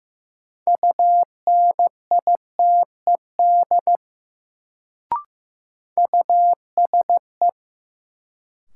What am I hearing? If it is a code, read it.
Morse code, heard as a keyed tone, about 15 wpm, 700 Hz. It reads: UNITED USE
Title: Sets of 2 Words - New Word Drill - 15wpm